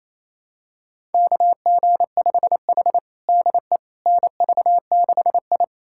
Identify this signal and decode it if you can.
Morse code, heard as a keyed tone, about 29 wpm, 700 Hz. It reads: KG5H DE NV6I
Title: USA Callsign DE Callsign - 4 Characters - 28wpm